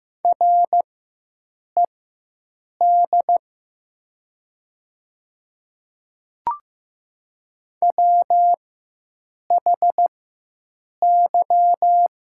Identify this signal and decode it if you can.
Morse code, heard as a keyed tone, about 15 wpm, spaced as for 7 wpm, 700 Hz. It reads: RED WHY
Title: 4x Character Spacing - Top 500 Words - 15wpm